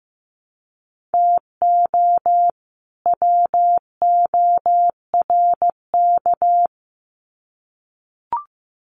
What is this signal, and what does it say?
Morse code, heard as a keyed tone, about 15 wpm, 700 Hz. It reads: TO WORK
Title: Sets of 2 Words 15wpm